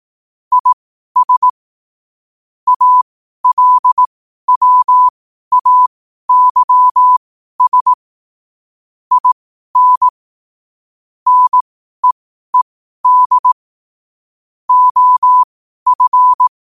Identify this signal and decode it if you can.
Morse code, heard as a keyed tone, about 18 wpm, spaced as for 12 wpm, 1000 Hz. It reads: IS ALWAYS IN NEED OF
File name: qotdfiles / 12wpm